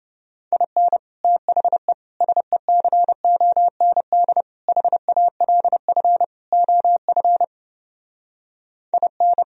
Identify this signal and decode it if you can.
Morse code, heard as a keyed tone, about 32 wpm, 700 Hz. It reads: IN THE SECOND HALF OF IN